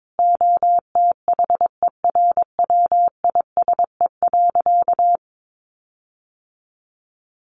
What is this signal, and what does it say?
Morse code, heard as a keyed tone, about 22 wpm, 700 Hz. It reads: OTHERWISE.
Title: Sentences from Top 1500 Words 22wpm